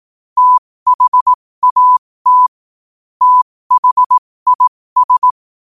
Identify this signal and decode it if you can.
Morse code, heard as a keyed tone, about 18 wpm, spaced as for 15 wpm, 1000 Hz. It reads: THAT THIS
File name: qotdfiles / 15wpm